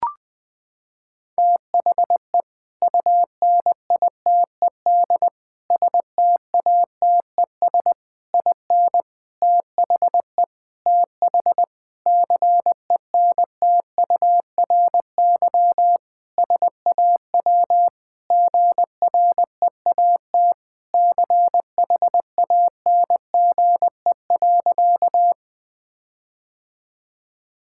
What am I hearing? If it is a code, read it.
Morse code, heard as a keyed tone, about 20 wpm, 700 Hz. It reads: THE UNITED STATES IN THE TH CENTURY SAW GREAT CHANGE.
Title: Sentences with 5-Word Phrases - 20wpm